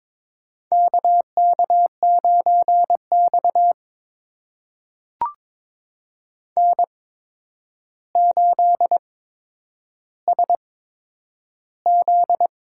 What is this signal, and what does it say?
Morse code, heard as a keyed tone, about 22 wpm, spaced as for 9 wpm, 700 Hz. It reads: KK9X N8SZ
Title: USA Call Signs - 4 Characters - 8x Character Spacing - 22wpm